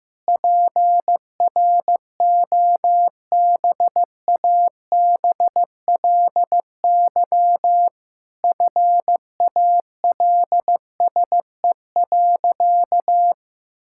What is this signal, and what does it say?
Morse code, heard as a keyed tone, about 15 wpm, 700 Hz. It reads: PROBABLY FALSE.
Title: Sentences from Top 1600 Words 15wpm